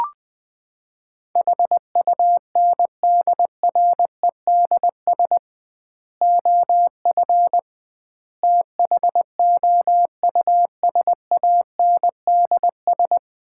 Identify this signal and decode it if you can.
Morse code, heard as a keyed tone, about 21 wpm, 700 Hz. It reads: HUNDREDS OF THOUSANDS
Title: 2x Word-Spacing - Sets of 4 Words - 20wpm